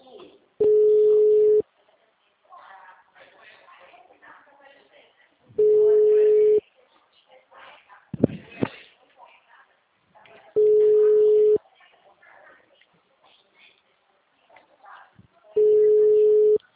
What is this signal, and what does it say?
Ringback tone (Europe, 425 Hz), with speech.